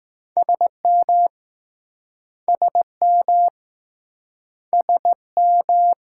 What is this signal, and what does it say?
Morse code, heard as a keyed tone, about 21 wpm, 700 Hz. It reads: SM SM SM